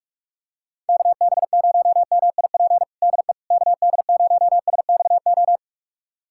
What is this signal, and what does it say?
Morse code, heard as a keyed tone, about 46 wpm, 700 Hz. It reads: KB0MIP DE KD0SXO
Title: USA Callsign DE Callsign 45wpm